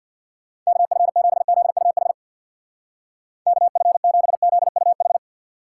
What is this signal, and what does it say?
Morse code, heard as a keyed tone, about 64 wpm, 700 Hz. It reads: KF7ZUH KF7ZUH